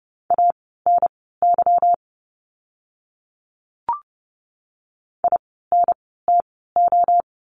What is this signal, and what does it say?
Morse code, heard as a keyed tone, about 30 wpm, spaced as for 17 wpm, 700 Hz. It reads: ANY INTO
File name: course / 30:10wpm / 237 Lesson 38f - Letter q - Words 30wpm